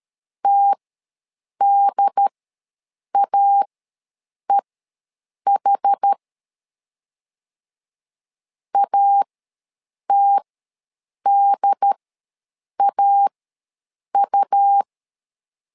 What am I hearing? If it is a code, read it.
Morse code, heard as a keyed tone, about 13 wpm, spaced as for 7 wpm, 800 Hz. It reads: TDAEH ATDAU
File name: LEZIONI / 2 UHDN / 2 UHD ws 12wpm 1722